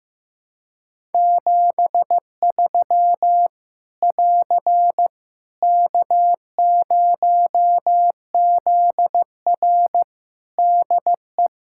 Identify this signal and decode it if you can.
Morse code, heard as a keyed tone, about 15 wpm, 700 Hz. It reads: 73 <AR> K0ZR DE
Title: Practice QSO - Exchange 4 - Ending - 15wpm